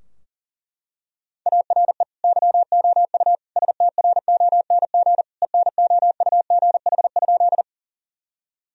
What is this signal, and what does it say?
Morse code, heard as a keyed tone, about 40 wpm, 700 Hz. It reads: ARE YOU STRONG ENOUGH?
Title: Sentences from Top 1000 Words - Review Part 3 - 40wpm